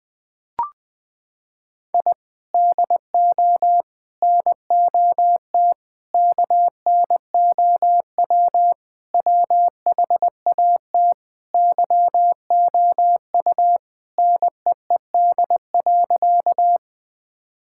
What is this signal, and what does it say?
Morse code, heard as a keyed tone, about 20 wpm, 700 Hz. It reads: I DO NOT KNOW WHAT YOU NEED.
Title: Sentences with 5-Word Phrases - 20wpm